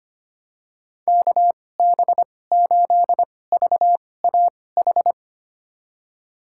Morse code, heard as a keyed tone, about 25 wpm, spaced as for 18 wpm, 700 Hz. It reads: KB8VAH
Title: USA Call Signs - 6 Characters - 2x Character Spacing - 25wpm